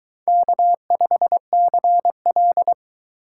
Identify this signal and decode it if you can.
Morse code, heard as a keyed tone, about 22 wpm, 700 Hz. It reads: K5CL